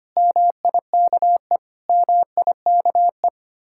Morse code, heard as a keyed tone, about 31 wpm, 700 Hz. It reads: MIKE MIKE
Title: Speed-Racing Common QSO Phrases - 25wpm